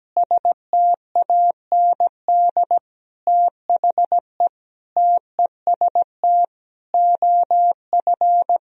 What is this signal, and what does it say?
Morse code, heard as a keyed tone, about 17 wpm, 700 Hz. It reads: STAND THE TEST OF
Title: Sentences from Top 1000 Words - Review Part 3 - 17wpm